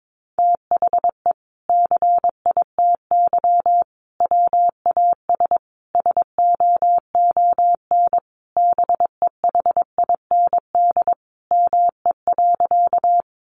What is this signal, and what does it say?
Morse code, heard as a keyed tone, about 22 wpm, 700 Hz. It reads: THE CITY WAS SOON BEHIND ME.
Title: Sentences from Top 500 Words 22wpm